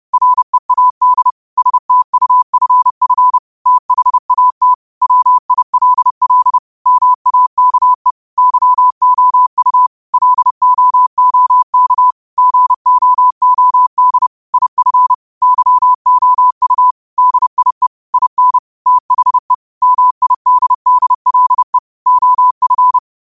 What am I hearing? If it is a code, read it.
Morse code, heard as a keyed tone, about 31 wpm, 1000 Hz. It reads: READ STUFF THAT WILL MAKE YOU LOOK GOOD IF YOU DIE IN THE MIDDLE OF